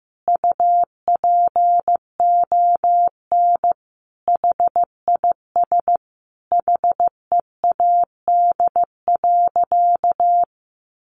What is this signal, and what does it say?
Morse code, heard as a keyed tone, about 15 wpm, 700 Hz. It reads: UPON HIS HEAD.
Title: Sentences from Top 1100 Words 15wpm